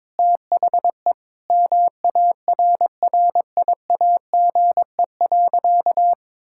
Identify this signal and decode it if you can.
Morse code, heard as a keyed tone, about 21 wpm, 700 Hz. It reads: THE MARRIAGE.
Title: Sentences from Top 1000 Words - Review Part 6 - 22wpm